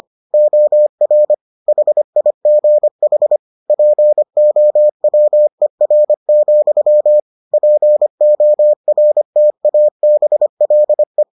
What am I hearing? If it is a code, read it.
Morse code, heard as a keyed tone, about 25 wpm, 600 Hz. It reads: OR HIGH POWER, PORTABLE